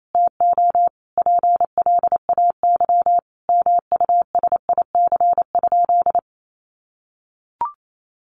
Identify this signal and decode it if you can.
Morse code, heard as a keyed tone, about 28 wpm, 700 Hz. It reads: TO PLAY MUSIC?
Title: Sentences from Top 1000 Words 28wpm